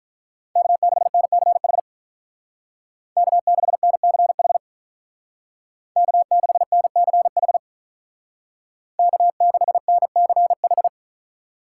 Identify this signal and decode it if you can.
Morse code, heard as a keyed tone, about 43 wpm, 700 Hz. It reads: K6NCH K6NCH K6NCH K6NCH
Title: Speed Racing - USA Call Signs 35wpm